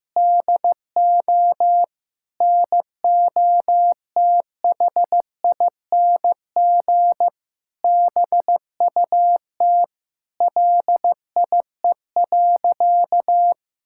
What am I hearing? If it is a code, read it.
Morse code, heard as a keyed tone, about 15 wpm, 700 Hz. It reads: DO NOTHING BUT LIE.
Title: Sentences from Top 1600 Words 15wpm